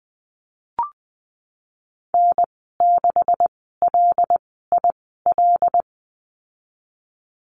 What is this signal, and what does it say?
Morse code, heard as a keyed tone, about 20 wpm, spaced as for 15 wpm, 700 Hz. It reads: N6LIL
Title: 193 Lesson 31g - Letter v - Call Signs 20wpm